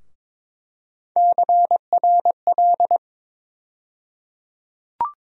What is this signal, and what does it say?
Morse code, heard as a keyed tone, about 21 wpm, 700 Hz. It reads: CRL
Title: USA Call Sign Postfixes - 3 Letters 22wpm